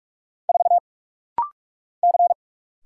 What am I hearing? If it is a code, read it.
Morse code, heard as a keyed tone, about 44 wpm, 690 Hz. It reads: 4 C
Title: Single Letter-Number - Rapid-Fire 45wpm